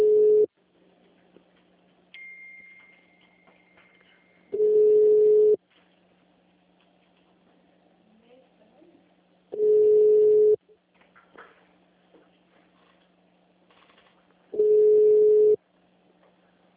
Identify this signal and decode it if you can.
Ringback tone (Europe, 425 Hz), with speech.